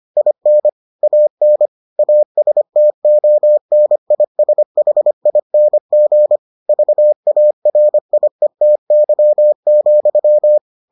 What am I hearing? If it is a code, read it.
Morse code, heard as a keyed tone, about 25 wpm, 600 Hz. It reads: IN AN ASTONISHING VARIETY,